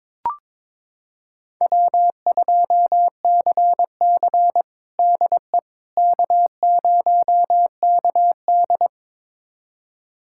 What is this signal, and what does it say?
Morse code, heard as a keyed tone, about 22 wpm, 700 Hz. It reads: W2CC DE K0KD
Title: USA Callsign DE Callsign - 4 Characters - 22wpm